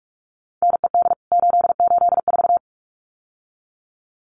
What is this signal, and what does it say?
Morse code, heard as a keyed tone, about 44 wpm, 700 Hz. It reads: NED 884
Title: CWT Exchange 45wpm